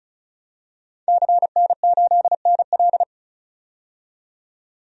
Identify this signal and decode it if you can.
Morse code, heard as a keyed tone, about 36 wpm, 700 Hz. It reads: CN8NL
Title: DXCC-100 Call Signs - 5 Characters - 35wpm